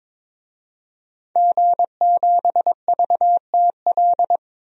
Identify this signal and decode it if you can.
Morse code, heard as a keyed tone, about 22 wpm, 700 Hz. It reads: G7VTL
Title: DXCC-100 Call Signs - 5 Characters - 22wpm